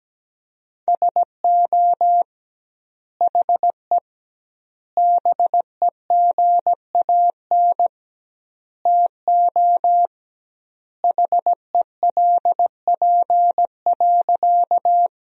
Morse code, heard as a keyed tone, about 17 wpm, 700 Hz. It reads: SO HE BEGAN TO HELP.
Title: 2x Word-Spacing - Sentences from Top 300 Words - 17wpm